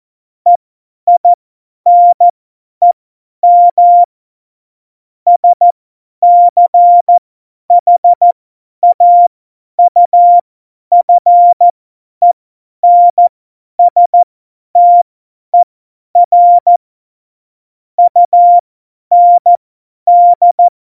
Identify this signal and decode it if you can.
Morse code, heard as a keyed tone, about 14 wpm, spaced as for 10 wpm, 700 Hz. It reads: EINEM SCHAUFENSTER UND
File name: Morsekurs / Witze / Witz 14 April 2020, 50 F70